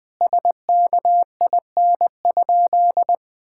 Morse code, heard as a keyed tone, about 21 wpm, 700 Hz. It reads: SKIN?
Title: Sentences from Top 1600 Words 20wpm